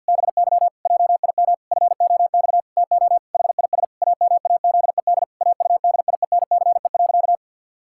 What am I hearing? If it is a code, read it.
Morse code, heard as a keyed tone, about 51 wpm, 700 Hz. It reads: BY JIM FOX TO HIS AMAZED AUDIENCE.
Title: Pangram Challenge v2 - 50wpm